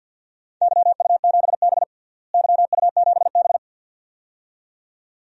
Morse code, heard as a keyed tone, about 46 wpm, 700 Hz. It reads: YU7B YU7B
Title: DXCC-100 Call Signs - 4 Characters - Repeated 2x - 50wpm